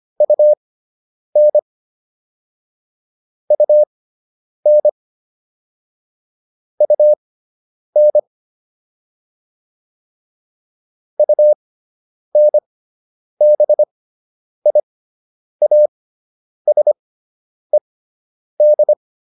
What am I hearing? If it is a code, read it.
Morse code, heard as a keyed tone, about 25 wpm, spaced as for 9 wpm, 600 Hz. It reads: UN UN UN UNBIASED